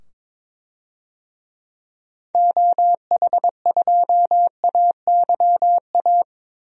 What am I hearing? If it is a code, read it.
Morse code, heard as a keyed tone, about 22 wpm, 700 Hz. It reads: OH2AYA
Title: DXCC-100 Call Signs - 6 Characters - 22wpm